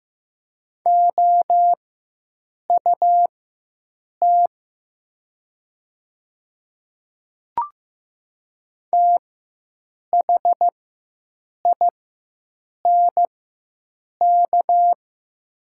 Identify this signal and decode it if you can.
Morse code, heard as a keyed tone, about 15 wpm, spaced as for 7 wpm, 700 Hz. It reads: OUT THINK
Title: 4x Character Spacing - Top 200 Words - 15wpm